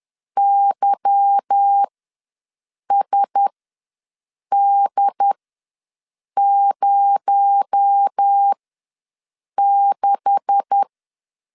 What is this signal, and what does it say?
Morse code, heard as a keyed tone, about 10 wpm, spaced as for 6 wpm, 800 Hz. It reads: YSD06
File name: LEZIONI / 10 SIGNS / 10A SLASH ws 10wpm 4121